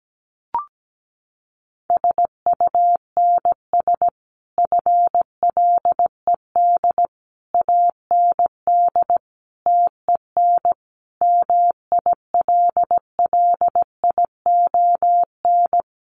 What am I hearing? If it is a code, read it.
Morse code, heard as a keyed tone, about 17 wpm, 700 Hz. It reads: SUNS FLED AND TEN MILLION
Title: A Sound Of Thunder 17wpm